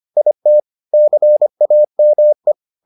Morse code, heard as a keyed tone, about 25 wpm, 600 Hz. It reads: IT CAME